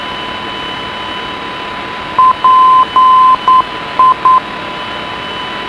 Morse code, heard as a keyed tone, about 9 wpm, 1030 Hz, with radio noise. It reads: PI